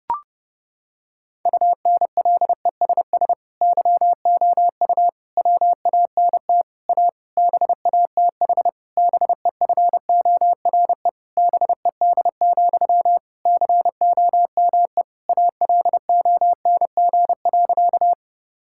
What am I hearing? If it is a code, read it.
Morse code, heard as a keyed tone, about 32 wpm, 700 Hz. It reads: UNLESS YOU WANT A BATH BEFORE BED, COME ALONG.
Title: Sentences from Top 1600 Words 30wpm